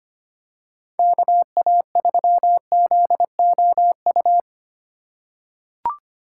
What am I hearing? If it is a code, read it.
Morse code, heard as a keyed tone, about 25 wpm, 700 Hz. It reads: KA3ZOU
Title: USA Call Signs - 6 Characters - 3x Character Spacing - 25wpm